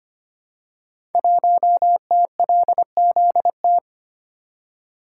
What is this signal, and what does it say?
Morse code, heard as a keyed tone, about 26 wpm, 700 Hz. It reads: 1TLZT
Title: Code Groups - 5 Letter:Numbers - 25wpm